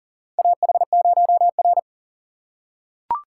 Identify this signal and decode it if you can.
Morse code, heard as a keyed tone, about 40 wpm, 700 Hz. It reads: AH0R
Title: USA Call Signs - 4 Characters - 2x Character Spacing - 40wpm